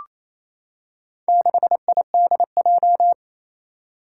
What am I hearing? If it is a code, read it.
Morse code, heard as a keyed tone, about 27 wpm, 700 Hz. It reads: 6IDJ
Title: Four Challenging Letter-Number Combinations 28wpm